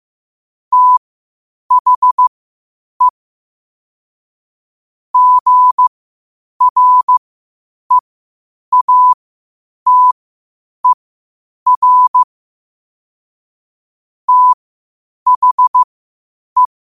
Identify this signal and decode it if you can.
Morse code, heard as a keyed tone, about 15 wpm, spaced as for 8 wpm, 1000 Hz. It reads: THE GREATER THE
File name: qotdfiles / 8wpm